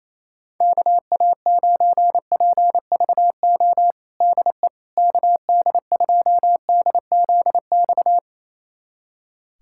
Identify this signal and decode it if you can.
Morse code, heard as a keyed tone, about 28 wpm, 700 Hz. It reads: KA9PVO DE KD2DZX
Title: USA Callsign DE Callsign - 6 Characters - 28wpm